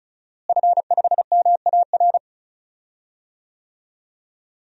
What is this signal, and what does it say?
Morse code, heard as a keyed tone, about 34 wpm, 700 Hz. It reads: F5MAR
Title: DXCC-100 Call Signs - 5 Characters - Repeated 2x - 35wpm